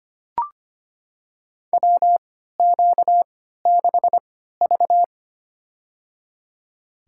Morse code, heard as a keyed tone, about 25 wpm, spaced as for 14 wpm, 700 Hz. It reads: WQ6V
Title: USA Call Signs - 4 Characters - 3x Character Spacing - 25wpm